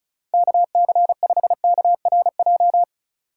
Morse code, heard as a keyed tone, about 34 wpm, 700 Hz. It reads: KC5KRJ
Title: USA Call Signs - 6 Characters - 2x Character Spacing - 35wpm